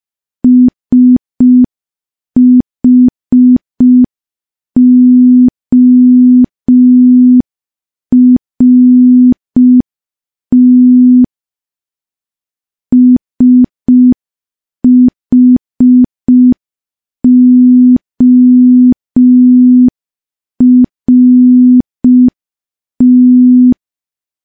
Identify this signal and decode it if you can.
Morse code, heard as a keyed tone, about 5 wpm, 250 Hz. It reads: SHORT SHORT
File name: morse / morse-training-05WPM